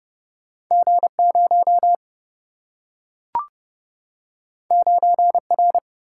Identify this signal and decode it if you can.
Morse code, heard as a keyed tone, about 30 wpm, 700 Hz. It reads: G 0 9 R
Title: Single-Letter Single-Number 30wpm